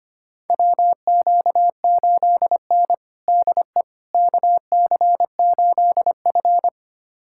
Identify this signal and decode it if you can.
Morse code, heard as a keyed tone, about 25 wpm, 700 Hz. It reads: WQ8N DE KC8F
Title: USA Callsign DE Callsign - 4 Characters - 25wpm